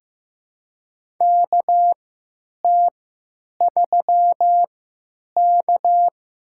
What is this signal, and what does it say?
Morse code, heard as a keyed tone, about 15 wpm, spaced as for 9 wpm, 700 Hz. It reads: KT3K
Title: USA Call Signs - 4 Characters - 3x Character Spacing - 15wpm